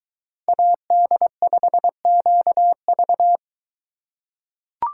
Morse code, heard as a keyed tone, about 23 wpm, 700 Hz. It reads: AD5QV